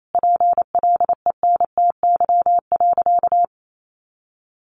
Morse code, heard as a keyed tone, about 28 wpm, 700 Hz. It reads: PLENTY.
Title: Sentences from Top 1300 Words 28wpm